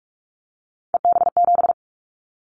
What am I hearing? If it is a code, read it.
Morse code, heard as a keyed tone, about 46 wpm, 700 Hz. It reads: EB7
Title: Callsign Prefixes for Top 25 DXCC - Repeated 3x - 45wpm